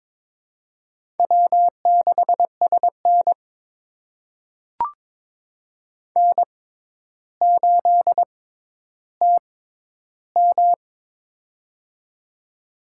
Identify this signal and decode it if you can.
Morse code, heard as a keyed tone, about 22 wpm, spaced as for 11 wpm, 700 Hz. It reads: W6SN N8TM
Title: USA Call Signs - 6x Character Spacing - 22wpm